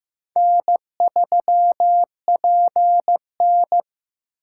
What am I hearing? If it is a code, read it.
Morse code, heard as a keyed tone, about 15 wpm, 700 Hz. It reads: N3PN
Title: USA Call Signs - 3x Character Spacing - 15wpm